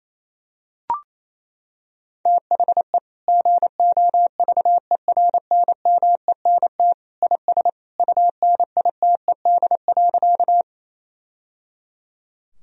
Morse code, heard as a keyed tone, about 28 wpm, 700 Hz. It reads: THE GOVERNMENT IS UNITED.
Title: Sentences from Top 1000 Words - Review Part 4 - 28wpm